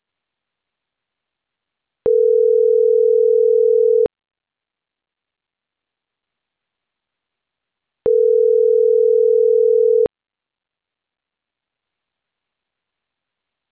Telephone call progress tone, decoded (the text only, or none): ringback tone